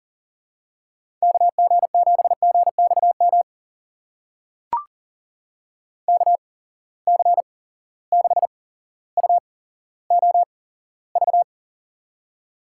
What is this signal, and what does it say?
Morse code, heard as a keyed tone, about 40 wpm, spaced as for 17 wpm, 700 Hz. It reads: KG7GXM KC6UOV
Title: USA Call Signs - 6 Characters - 8x Character Spacing - 40wpm